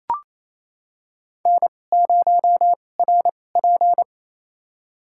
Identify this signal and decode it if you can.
Morse code, heard as a keyed tone, about 28 wpm, spaced as for 20 wpm, 700 Hz. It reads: N0RP
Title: USA Call Signs - 4 Characters - 2x Character Spacing - 28wpm